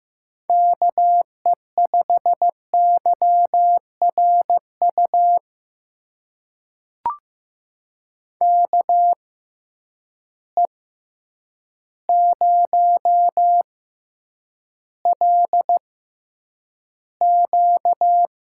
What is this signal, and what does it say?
Morse code, heard as a keyed tone, about 15 wpm, spaced as for 8 wpm, 700 Hz. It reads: KE5YRU KE0LQ
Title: USA Call Signs - 6x Character Spacing - 15wpm